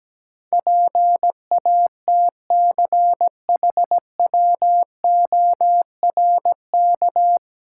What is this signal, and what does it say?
Morse code, heard as a keyed tone, about 17 wpm, 700 Hz. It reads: PATCHWORK
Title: Pangram Challenge v2 - 17wpm